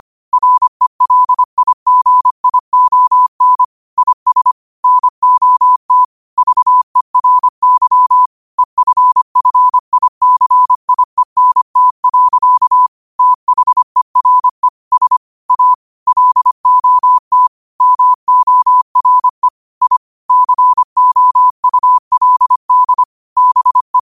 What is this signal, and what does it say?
Morse code, heard as a keyed tone, about 25 wpm, 1000 Hz. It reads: RELIGION IS NOT VERY EFFICIENT. THERE S A LOT MORE I COULD BE